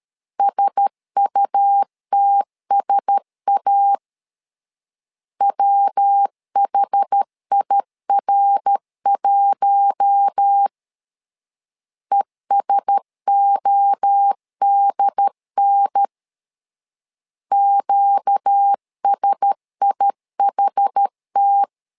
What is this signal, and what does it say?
Morse code, heard as a keyed tone, about 13 wpm, 800 Hz. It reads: SUTSA WHIR1 ESODN QSIHT
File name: LEZIONI / 4 Q1S 12wpm 1802